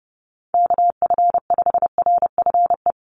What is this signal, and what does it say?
Morse code, heard as a keyed tone, about 30 wpm, 700 Hz. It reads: KF5RFE
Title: USA Call Signs 30wpm